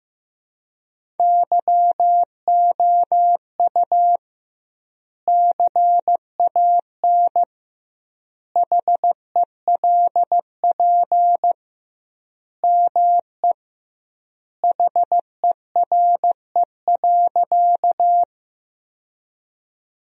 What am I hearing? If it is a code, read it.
Morse code, heard as a keyed tone, about 15 wpm, 700 Hz. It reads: YOU CAN HELP ME HERE.
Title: 2x Word-Spacing - Sentences from Top 200 Words - 15wpm